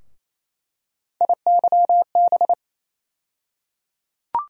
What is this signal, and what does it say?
Morse code, heard as a keyed tone, about 28 wpm, 700 Hz. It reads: IYB
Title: USA Call Sign Postfixes - 3 Letters 28wpm